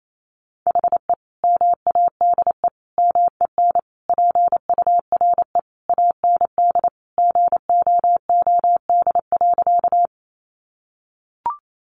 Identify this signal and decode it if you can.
Morse code, heard as a keyed tone, about 28 wpm, 700 Hz. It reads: HE MADE MEN PURE AND GOOD.
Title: Sentences from Top 1500 Words 28wpm